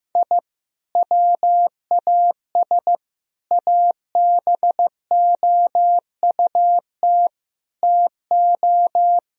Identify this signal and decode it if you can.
Morse code, heard as a keyed tone, about 15 wpm, 700 Hz. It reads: I WAS ABOUT TO
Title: Sentences from Top 1000 Words - Review Part 1 - 15wpm